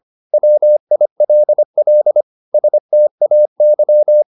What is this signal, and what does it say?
Morse code, heard as a keyed tone, about 25 wpm, 600 Hz. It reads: WILL STAY
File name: the old man and the sea part 1 25wpm